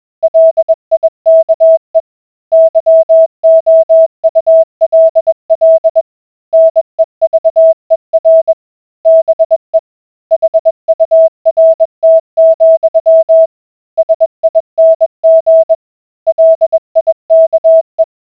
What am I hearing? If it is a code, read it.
Morse code, heard as a keyed tone, about 21 wpm, 650 Hz. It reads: LIKE YOULL NEVER BE HURT, SING LIKE